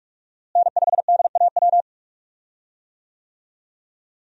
Morse code, heard as a keyed tone, about 42 wpm, 700 Hz. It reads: N5DAW
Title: USA Call Signs - 5 Characters - Repeated 3x - 45wpm